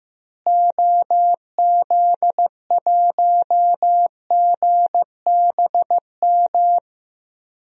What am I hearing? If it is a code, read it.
Morse code, heard as a keyed tone, about 15 wpm, 700 Hz. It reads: OZ1GBM
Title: DXCC-100 Call Signs - 6 Characters - Repeated 3x - 15wpm